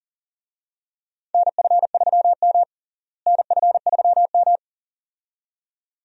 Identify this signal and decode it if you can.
Morse code, heard as a keyed tone, about 40 wpm, 700 Hz. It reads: NF3M NF3M
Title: USA Call Signs - 4 Characters - Repeated 2x - 40wpm